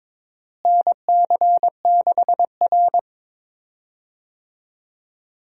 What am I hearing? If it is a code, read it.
Morse code, heard as a keyed tone, about 22 wpm, 700 Hz. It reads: NC6R
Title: USA Call Signs - 4 Characters - Repeated 3x - 22wpm